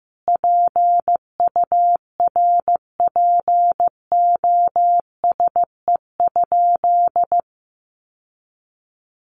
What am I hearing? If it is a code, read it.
Morse code, heard as a keyed tone, about 15 wpm, 700 Hz. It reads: PURPOSE?
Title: Sentences from Top 1500 Words 15wpm